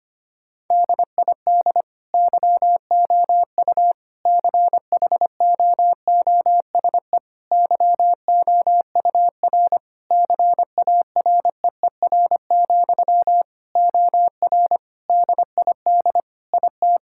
Morse code, heard as a keyed tone, about 25 wpm, 700 Hz. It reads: DID YOU CHOOSE YOUR CAREER, OR DID IT
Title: Sentences from Top 1000 Words - Review Part 6 - 25wpm